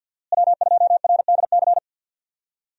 Morse code, heard as a keyed tone, about 52 wpm, 700 Hz. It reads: W2RDC
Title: USA Call Signs - 5 Characters - Repeated 3x - 50wpm